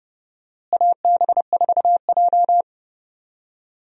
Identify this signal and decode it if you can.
Morse code, heard as a keyed tone, about 30 wpm, 700 Hz. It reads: AB4J
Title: USA Call Signs - 4 Characters - Repeated 3x - 30wpm